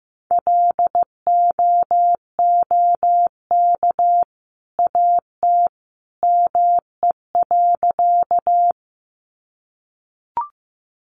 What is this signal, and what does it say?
Morse code, heard as a keyed tone, about 15 wpm, 700 Hz. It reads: LOOK AT ME.
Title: Sentences from Top 200 Words 15wpm